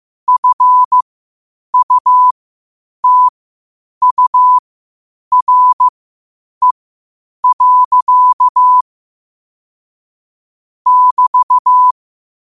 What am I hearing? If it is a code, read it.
Morse code, heard as a keyed tone, about 15 wpm, spaced as for 8 wpm, 1000 Hz. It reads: FUTURE. <BT>
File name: qotdfiles / 8wpm